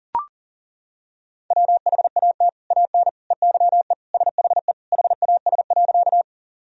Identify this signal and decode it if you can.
Morse code, heard as a keyed tone, about 37 wpm, 700 Hz. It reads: WHAT AN EYE SHE HAS.
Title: Sentences from Top 1000 Words - Review Part 1 - 40wpm